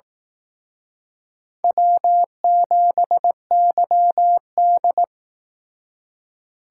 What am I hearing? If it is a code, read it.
Morse code, heard as a keyed tone, about 18 wpm, 700 Hz. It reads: W7YD
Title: Speed Racing - USA 1x2 Call Signs 15wpm